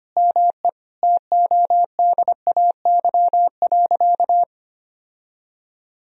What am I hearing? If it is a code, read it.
Morse code, heard as a keyed tone, about 25 wpm, 700 Hz. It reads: ME TODAY.
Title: Sentences from Top 1000 Words - Review Part 4 - 25wpm